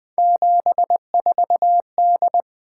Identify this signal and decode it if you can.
Morse code, heard as a keyed tone, about 20 wpm, 700 Hz. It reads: 74D
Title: Three Letter-Number Combinations - 20wpm